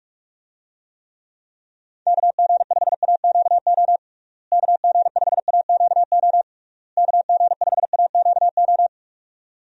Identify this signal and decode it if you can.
Morse code, heard as a keyed tone, about 46 wpm, 700 Hz. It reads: KG5AQO KG5AQO KG5AQO
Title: USA Call Signs - 6 Characters - Repeated 3x - 45wpm